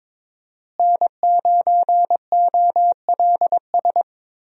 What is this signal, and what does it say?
Morse code, heard as a keyed tone, about 22 wpm, 700 Hz. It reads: N9OLS
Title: USA Call Signs - 5 Characters - Repeated 2x - 22wpm